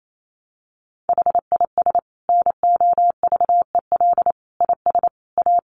Morse code, heard as a keyed tone, about 28 wpm, 700 Hz. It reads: HIS NOVEL IS A
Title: Sentences from Top 1300 Words 28wpm